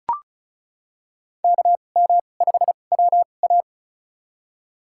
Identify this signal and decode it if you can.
Morse code, heard as a keyed tone, about 36 wpm, spaced as for 25 wpm, 700 Hz. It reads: KM5WA
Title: USA Call Signs - 2x Character Spacing - 35wpm